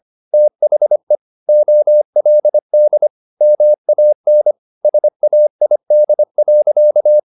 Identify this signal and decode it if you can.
Morse code, heard as a keyed tone, about 25 wpm, 600 Hz. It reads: THE OLD MAN SAID.
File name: the old man and the sea part 1 25wpm